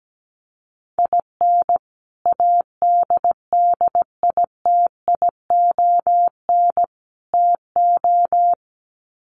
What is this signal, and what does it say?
Morse code, heard as a keyed tone, about 17 wpm, 700 Hz. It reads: IN ADDITION TO
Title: Sets of 3 Words 17wpm